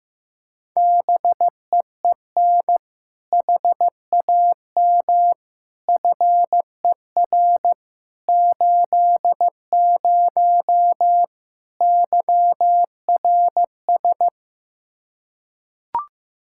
Morse code, heard as a keyed tone, about 15 wpm, 700 Hz. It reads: BEEN HAM FER 80 YRS
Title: K1USN SST QSO Practice 15wpm